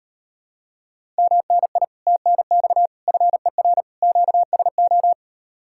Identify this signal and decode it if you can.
Morse code, heard as a keyed tone, about 38 wpm, 700 Hz. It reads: MNI TNX FER QSO